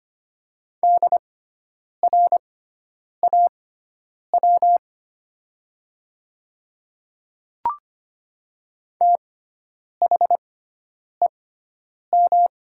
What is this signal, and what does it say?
Morse code, heard as a keyed tone, about 25 wpm, spaced as for 9 wpm, 700 Hz. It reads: DRAW THEM
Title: 6x Character Spacing - Top 300 Words - 25wpm